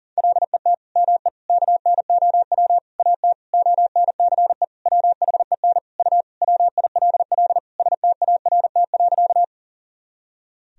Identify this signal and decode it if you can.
Morse code, heard as a keyed tone, about 40 wpm, 700 Hz. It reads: LET ME KNOW AT ONCE WHEN U WILL START.
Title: Sentences from Top 300 Words and Abbreviations 40wpm